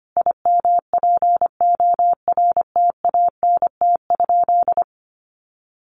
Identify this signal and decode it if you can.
Morse code, heard as a keyed tone, about 25 wpm, 700 Hz. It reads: IMPORTANT?
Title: Sentences from Top 1300 Words 25wpm